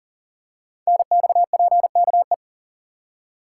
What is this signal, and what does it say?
Morse code, heard as a keyed tone, about 40 wpm, 700 Hz. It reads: NXPKE
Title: Code Groups - 5 Letters - 40wpm